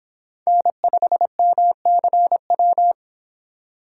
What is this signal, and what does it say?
Morse code, heard as a keyed tone, about 32 wpm, 700 Hz. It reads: N5MCW